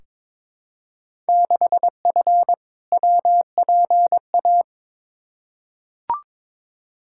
Morse code, heard as a keyed tone, about 21 wpm, 700 Hz. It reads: 6F WPA
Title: Field Day Exchanges Encore - 22wpm